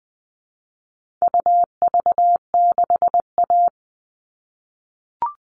Morse code, heard as a keyed tone, about 20 wpm, 700 Hz. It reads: UV6A
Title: All Cut Numbers - 4-Digit 20wpm